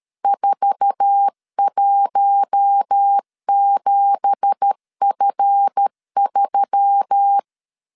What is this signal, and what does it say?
Morse code, heard as a keyed tone, about 13 wpm, 800 Hz. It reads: 417F3
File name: LEZIONI / 10 SIGNS / 10A SLASH 12wpm 2314